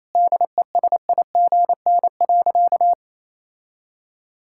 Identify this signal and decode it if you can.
Morse code, heard as a keyed tone, about 28 wpm, 700 Hz. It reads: DESIGN.
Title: Sentences from Top 1000 Words - Review Part 5 - 28wpm